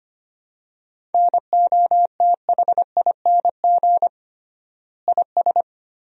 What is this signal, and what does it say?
Morse code, heard as a keyed tone, about 25 wpm, 700 Hz. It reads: NOTHING IS
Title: 3x Word-Spacing - Sentences from Top 500 Words - 25wpm